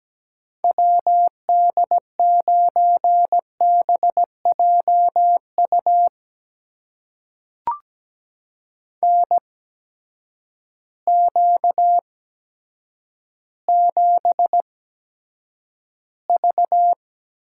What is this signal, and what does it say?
Morse code, heard as a keyed tone, about 17 wpm, spaced as for 8 wpm, 700 Hz. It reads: WD9BJU NQ7V
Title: USA Call Signs - 8x Character Spacing - 17wpm